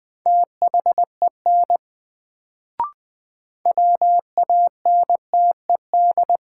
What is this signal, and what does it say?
Morse code, heard as a keyed tone, about 20 wpm, 700 Hz. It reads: THEN WANTED
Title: Sets of 2 Words - New Word Drill - 20wpm